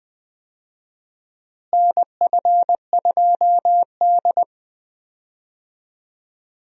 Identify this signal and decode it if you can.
Morse code, heard as a keyed tone, about 21 wpm, 700 Hz. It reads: NF2D
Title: USA Call Signs - 4 Characters - Repeated 3x - 20wpm